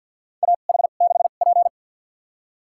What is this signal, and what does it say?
Morse code, heard as a keyed tone, about 51 wpm, spaced as for 36 wpm, 700 Hz. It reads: AH6P
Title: USA Call Signs - 4 Characters - 2x Character Spacing - 50wpm